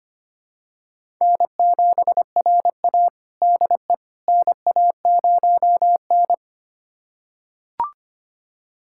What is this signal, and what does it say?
Morse code, heard as a keyed tone, about 25 wpm, 700 Hz. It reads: N7RA DE NA0N
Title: USA Callsign DE Callsign - 4 Characters - 25wpm